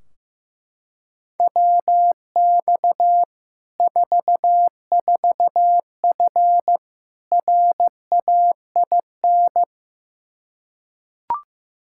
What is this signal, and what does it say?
Morse code, heard as a keyed tone, about 15 wpm, 700 Hz. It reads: WX 44F RAIN
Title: Practice QSO - Exchange 2 - Info - 15wpm